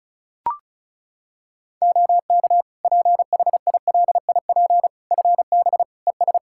Morse code, heard as a keyed tone, about 34 wpm, 700 Hz. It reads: OK PHILIP FB ES
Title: K1USN SST QSO Practice 35wpm